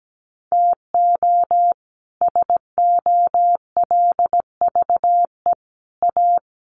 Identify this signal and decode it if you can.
Morse code, heard as a keyed tone, about 17 wpm, 700 Hz. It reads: TO SOLVE A
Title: Sentences from Top 1400 Words 17wpm